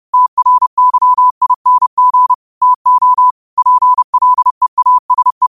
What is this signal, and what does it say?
Morse code, heard as a keyed tone, about 30 wpm, 1000 Hz. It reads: TRYING TO PLEASE
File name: qotdfiles / 30wpm